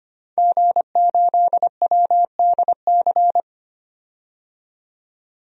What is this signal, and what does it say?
Morse code, heard as a keyed tone, about 25 wpm, 690 Hz. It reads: G8WDC
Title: DXCC-100 Call Signs - 5 Characters - 25wpm